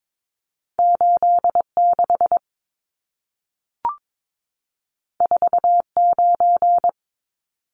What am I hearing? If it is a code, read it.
Morse code, heard as a keyed tone, about 22 wpm, 700 Hz. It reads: 86 49
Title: Numbers - 2-Digit 22wpm